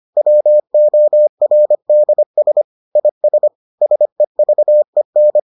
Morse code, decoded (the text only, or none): WORDS IS SEVEN